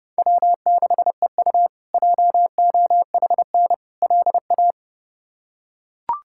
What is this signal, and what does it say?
Morse code, heard as a keyed tone, about 31 wpm, 700 Hz. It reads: W6EU JOHN LA
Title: K1USN SST QSO Practice - 30wpm